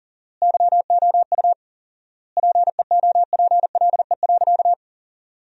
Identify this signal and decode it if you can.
Morse code, heard as a keyed tone, about 37 wpm, 700 Hz. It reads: YOU PEOPLE.
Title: 4x Word-Spacing - Sentences from Top 300 Words - 40wpm